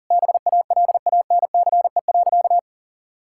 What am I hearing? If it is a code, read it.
Morse code, heard as a keyed tone, about 40 wpm, 700 Hz. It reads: BALANCE.
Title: Sentences from Top 1000 Words - Review Part 6 - 40wpm